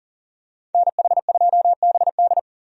Morse code, heard as a keyed tone, about 39 wpm, 700 Hz. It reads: NH2BD
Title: USA Call Signs - 5 Characters - 40wpm